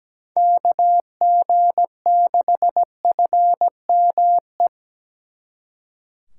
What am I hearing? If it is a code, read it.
Morse code, heard as a keyed tone, about 20 wpm, 700 Hz. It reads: KG6FME
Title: Speed Racing - USA 2x3 Call Signs 17wpm